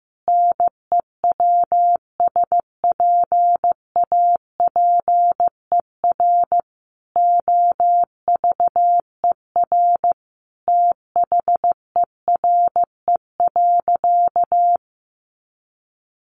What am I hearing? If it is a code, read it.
Morse code, heard as a keyed tone, about 15 wpm, 700 Hz. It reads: NEWSPAPER OVER THERE.
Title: Sentences from Top 1100 Words 15wpm